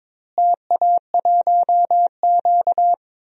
Morse code, heard as a keyed tone, about 21 wpm, 700 Hz. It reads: TA1Q
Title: DXCC-100 Call Signs - 4 Characters - Repeated 2x - 22wpm